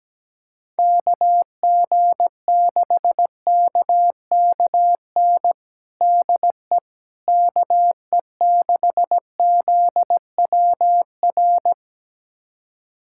Morse code, heard as a keyed tone, about 17 wpm, 700 Hz. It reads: KG6KKN DE KE6ZWR
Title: USA Callsign DE Callsign - 6 Characters - 17wpm